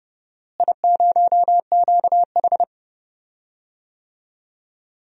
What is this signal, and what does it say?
Morse code, heard as a keyed tone, about 30 wpm, 700 Hz. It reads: I0QH
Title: DXCC-100 Call Signs - 4 Characters - Repeated 3x - 30wpm